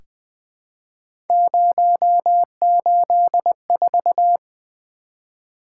Morse code, heard as a keyed tone, about 20 wpm, 700 Hz. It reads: 084
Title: Three Letter-Number Combinations - 20wpm